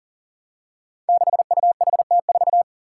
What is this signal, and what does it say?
Morse code, heard as a keyed tone, about 40 wpm, 700 Hz. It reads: 6UHT4
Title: Five Challenging Letter-Number Combinations 40wpm